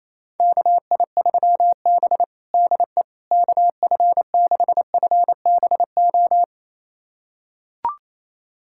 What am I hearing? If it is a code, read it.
Morse code, heard as a keyed tone, about 28 wpm, 700 Hz. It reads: KI3B DE KF6FBO
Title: USA Callsign DE Callsign 28wpm